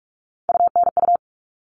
Morse code, heard as a keyed tone, about 44 wpm, 700 Hz. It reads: UNU
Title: All Cut Numbers - 3-Digit 45wpm